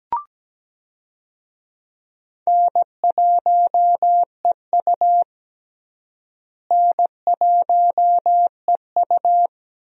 Morse code, heard as a keyed tone, about 17 wpm, 700 Hz. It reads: N1EU N1EU
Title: USA Call Signs - 4 Characters - Repeated 2x - 17wpm